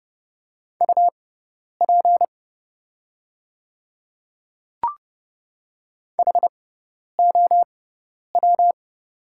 Morse code, heard as a keyed tone, about 31 wpm, spaced as for 10 wpm, 700 Hz. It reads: UP HOW
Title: 6x Character Spacing - Top 100 Words - 30wpm